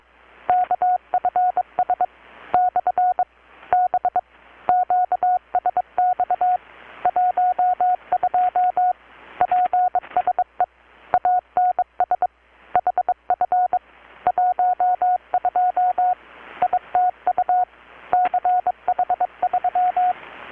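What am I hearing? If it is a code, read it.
Morse code, heard as a keyed tone, about 22 wpm, 720 Hz, with radio noise. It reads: KFS / B QSX 12 PSE ANS HF 12 ITU CH3
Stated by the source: Morse from KPH's Night of Nights XX maritime radiotelegraphy event, received in Chiapas, Mexico about 3,800 km away on a Kenwood R-600 with a grounded 10 m long-wire antenna.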